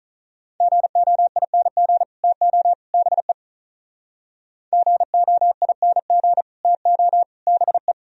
Morse code, heard as a keyed tone, about 45 wpm, 690 Hz. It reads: GOING TO BE GOING TO BE